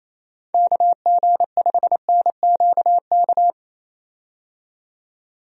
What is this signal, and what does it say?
Morse code, heard as a keyed tone, about 27 wpm, 700 Hz. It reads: KG5NQK
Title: USA Call Signs - 6 Characters - 28wpm